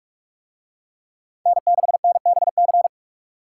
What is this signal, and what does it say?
Morse code, heard as a keyed tone, about 41 wpm, 700 Hz. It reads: N6NBC